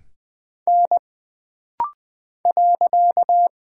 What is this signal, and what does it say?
Morse code, heard as a keyed tone, about 20 wpm, 700 Hz. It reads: N .
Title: Single Character (Letter:Number:Punctuation:Prosign) - Rapid-Fire - 20wpm